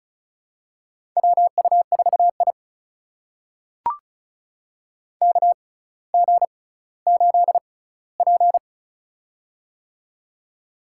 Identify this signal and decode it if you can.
Morse code, heard as a keyed tone, about 36 wpm, spaced as for 18 wpm, 700 Hz. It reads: WU4I KG8P
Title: USA Call Signs - 4 Characters - 6x Character Spacing - 35wpm